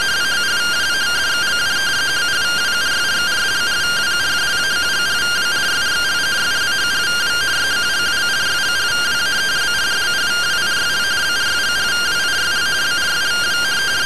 Two-tone teleprinter signal, about 170 Hz shift.